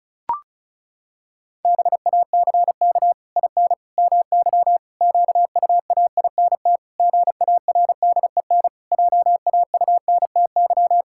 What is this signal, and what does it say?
Morse code, heard as a keyed tone, about 36 wpm, 700 Hz. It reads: BACK IN MY QUAINT GARDEN JAUNTY
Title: Pangram Challenge v2 - 35wpm